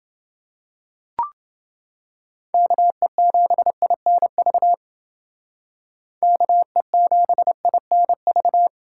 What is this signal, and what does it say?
Morse code, heard as a keyed tone, about 29 wpm, 700 Hz. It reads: KE7INV KE7INV